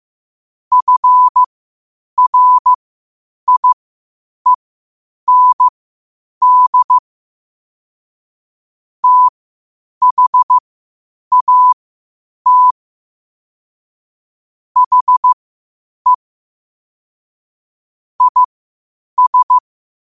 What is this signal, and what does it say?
Morse code, heard as a keyed tone, about 15 wpm, spaced as for 8 wpm, 1000 Hz. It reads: FRIEND THAT HE IS
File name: qotdfiles / 8wpm